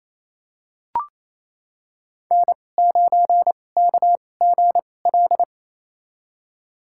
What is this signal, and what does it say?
Morse code, heard as a keyed tone, about 28 wpm, spaced as for 20 wpm, 700 Hz. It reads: N9KGL
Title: USA Call Signs - 2x Character Spacing - 28wpm